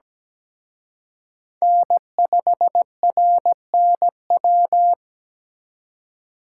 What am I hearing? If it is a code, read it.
Morse code, heard as a keyed tone, about 20 wpm, 700 Hz. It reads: N5RNW
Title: Speed Racing - USA Call Signs 17wpm